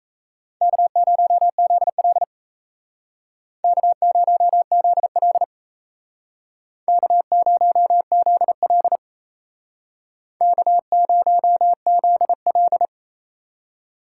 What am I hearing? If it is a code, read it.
Morse code, heard as a keyed tone, about 35 wpm, 700 Hz. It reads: K0ZL K0ZL K0ZL K0ZL